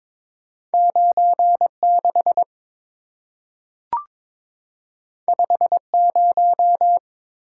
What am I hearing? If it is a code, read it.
Morse code, heard as a keyed tone, about 22 wpm, 690 Hz. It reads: 96 50
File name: practice / 22wpm / Two Letter-Number Combinations - 22wpm